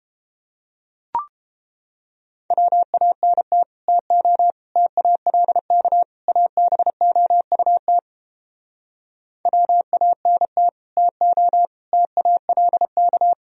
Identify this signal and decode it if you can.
Morse code, heard as a keyed tone, about 29 wpm, 700 Hz. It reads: WANT TO TALK ABOUT WANT TO TALK